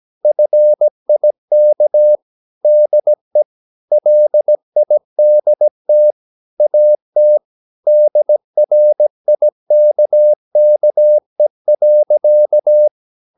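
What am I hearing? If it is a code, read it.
Morse code, heard as a keyed tone, about 17 wpm, 600 Hz. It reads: FIK DE LIDT AT DRIKKE.
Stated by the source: Slow (QRS) Morse practice at 17 wpm, from CW med Gnister.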